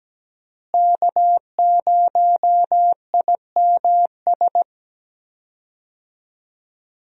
Morse code, heard as a keyed tone, about 17 wpm, 700 Hz. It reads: K0IMS
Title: USA Call Signs - 5 Characters - Repeated 2x - 17wpm